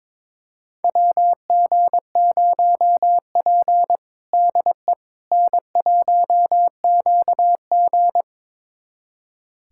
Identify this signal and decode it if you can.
Morse code, heard as a keyed tone, about 22 wpm, 700 Hz. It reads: WG0P DE N1QG
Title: USA Callsign DE Callsign - 4 Characters - 22wpm